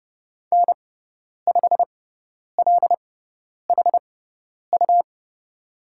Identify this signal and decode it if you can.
Morse code, heard as a keyed tone, about 30 wpm, spaced as for 10 wpm, 700 Hz. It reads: N5LHU
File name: Lesson 17g - Letter u - Call Signs 30wpm